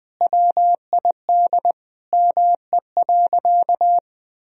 Morse code, heard as a keyed tone, about 20 wpm, 700 Hz. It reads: WID ME.
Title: Sentences from Top 500 Words and Abbreviations 20wpm